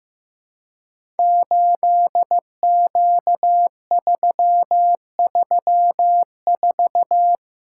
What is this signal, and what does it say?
Morse code, heard as a keyed tone, about 15 wpm, 700 Hz. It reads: 8Q334
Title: Code Groups - 5 Letter:Numbers - 15wpm